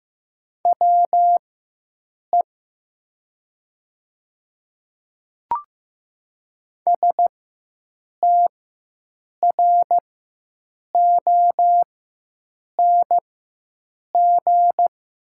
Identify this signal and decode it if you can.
Morse code, heard as a keyed tone, about 15 wpm, spaced as for 7 wpm, 700 Hz. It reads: WE STRONG
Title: 4x Character Spacing - Top 500 Words - 15wpm